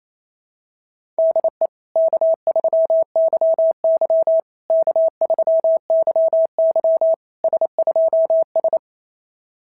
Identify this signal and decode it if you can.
Morse code, heard as a keyed tone, about 29 wpm, 690 Hz. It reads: DE K3YY K3YY S2S
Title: SOTA QSO Practice 28wpm